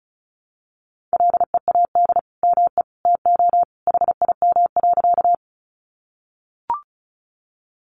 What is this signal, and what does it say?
Morse code, heard as a keyed tone, about 34 wpm, 700 Hz. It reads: LEAD ME TO HIM.
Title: Sentences from Top 500 Words 35wpm